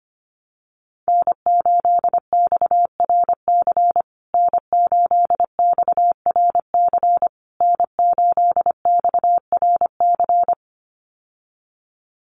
Morse code, heard as a keyed tone, about 25 wpm, 700 Hz. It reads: N8XRC N8XRC N8XRC
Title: USA Call Signs Repeated 3x 25wpm